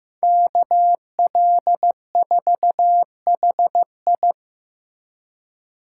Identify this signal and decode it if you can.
Morse code, heard as a keyed tone, about 15 wpm, 700 Hz. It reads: KL4HI
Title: USA Call Signs - 5 Characters - 2x Character Spacing - 15wpm